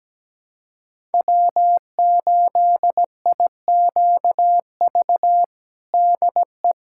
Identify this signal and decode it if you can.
Morse code, heard as a keyed tone, about 17 wpm, 700 Hz. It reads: W8IQV DE
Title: USA Callsign DE Callsign 17wpm